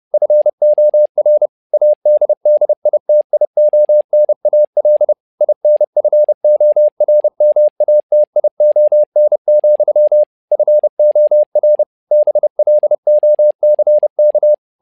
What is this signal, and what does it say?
Morse code, heard as a keyed tone, about 30 wpm, 600 Hz. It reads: FOR ADDITIONAL INFORMATION, FOR BLOCK
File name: learning center / tower trouble / tower trouble 30wpm